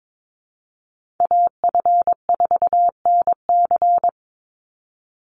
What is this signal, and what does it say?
Morse code, heard as a keyed tone, about 22 wpm, 700 Hz. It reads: AF4NC
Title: Call Signs - 22wpm